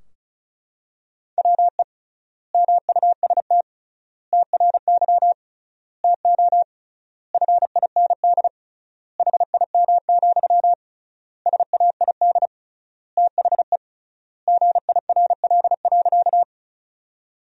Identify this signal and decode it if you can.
Morse code, heard as a keyed tone, about 36 wpm, 700 Hz. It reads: WE MUST TRY TO FIND HIM, SAID THE GIRL.
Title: 3x Word-Spacing - Sentences from Top 400 Words - 35wpm